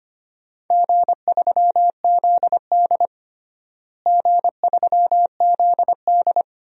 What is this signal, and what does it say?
Morse code, heard as a keyed tone, about 25 wpm, 700 Hz. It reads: G3ZD G3ZD
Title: DXCC-100 Call Signs - 4 Characters - Repeated 3x - 25wpm